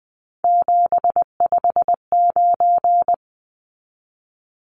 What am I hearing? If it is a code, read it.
Morse code, heard as a keyed tone, about 20 wpm, 700 Hz. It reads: 759
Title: Numbers - 3-Digit 20wpm